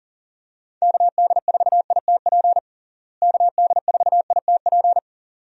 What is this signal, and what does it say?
Morse code, heard as a keyed tone, about 37 wpm, 700 Hz. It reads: KD4ITP KD4ITP
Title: USA Call Signs - 6 Characters - Repeated 2x - 40wpm